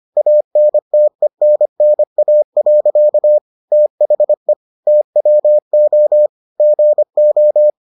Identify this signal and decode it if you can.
Morse code, heard as a keyed tone, about 25 wpm, 600 Hz. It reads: ANTENNA. THE TWO GO